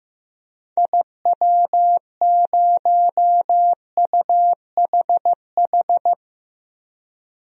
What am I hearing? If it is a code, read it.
Morse code, heard as a keyed tone, about 15 wpm, 700 Hz. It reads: IW0UHH
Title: DXCC-100 Call Signs - 6 Characters - 15wpm